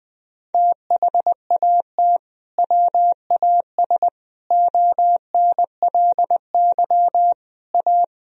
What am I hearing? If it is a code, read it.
Morse code, heard as a keyed tone, about 20 wpm, 700 Hz. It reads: THAT WAS ONLY A
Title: Sentences from Top 1000 Words - Review Part 1 - 20wpm